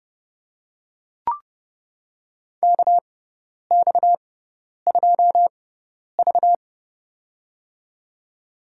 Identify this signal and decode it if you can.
Morse code, heard as a keyed tone, about 30 wpm, spaced as for 10 wpm, 700 Hz. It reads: KX2V
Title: USA Call Signs - 4 Characters - 6x Character Spacing - 30wpm